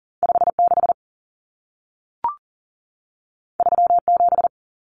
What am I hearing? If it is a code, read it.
Morse code, heard as a keyed tone, about 40 wpm, 700 Hz. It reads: 56 37
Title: Numbers - 2-Digit 40wpm